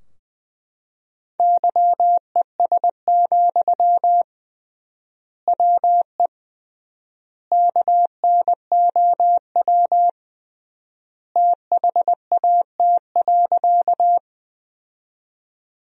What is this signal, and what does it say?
Morse code, heard as a keyed tone, about 20 wpm, 700 Hz. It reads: YES, WE KNOW THAT.
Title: 3x Word-Spacing - Sentences from Top 500 Words - 20wpm